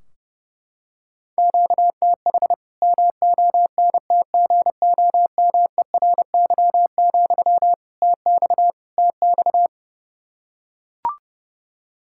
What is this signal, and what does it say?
Morse code, heard as a keyed tone, about 30 wpm, 700 Hz. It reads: QTH MONTGOMERY, TX TX
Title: K1USN SST QSO Practice 30wpm